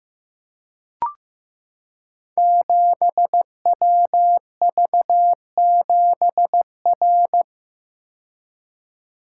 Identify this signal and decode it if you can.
Morse code, heard as a keyed tone, about 15 wpm, 700 Hz. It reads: 7WV7R
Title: Code Groups - 5 Letter:Numbers - 15wpm